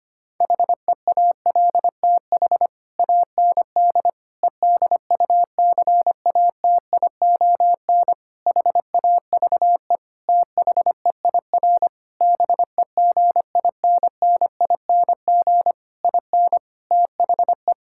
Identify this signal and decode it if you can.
Morse code, heard as a keyed tone, about 25 wpm, 700 Hz. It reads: HEALTH AND EDUCATION HAVE THEIR BEGINNING IN THE
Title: Sentences from Top 1000 Words - Review Part 6 - 25wpm